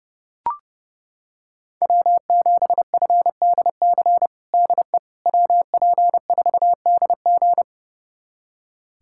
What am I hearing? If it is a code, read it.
Morse code, heard as a keyed tone, about 30 wpm, 700 Hz. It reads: W7FDC DE WP4DG
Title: USA Callsign DE Callsign - 5 Characters - 30wpm